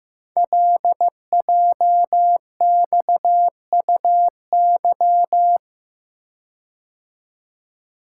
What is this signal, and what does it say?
Morse code, heard as a keyed tone, about 15 wpm, 700 Hz. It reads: LJXUY
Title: Code Groups - 5 Letter:Numbers - 15wpm